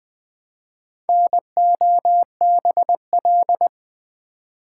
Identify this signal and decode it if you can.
Morse code, heard as a keyed tone, about 20 wpm, 700 Hz. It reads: NOBL
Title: Build A Word - 1 Letter At A Time - 5-Letter Words - 20wpm